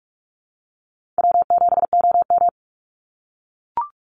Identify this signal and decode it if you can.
Morse code, heard as a keyed tone, about 44 wpm, 700 Hz. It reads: W7OM
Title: Call Signs - 45wpm